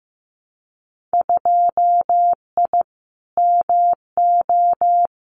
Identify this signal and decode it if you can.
Morse code, heard as a keyed tone, about 15 wpm, 700 Hz. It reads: 2I MO
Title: Winter Field Day Exchanges 15wpm